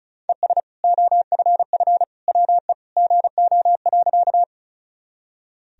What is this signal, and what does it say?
Morse code, heard as a keyed tone, about 34 wpm, 700 Hz. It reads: ES OFF WE GO.
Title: Sentences from Top 200 Words and Abbreviations 35wpm